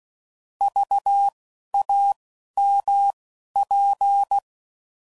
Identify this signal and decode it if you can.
Morse code, heard as a keyed tone, about 16 wpm, spaced as for 11 wpm, 800 Hz. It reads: VAMP
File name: Prefixes / By First Letter / Vs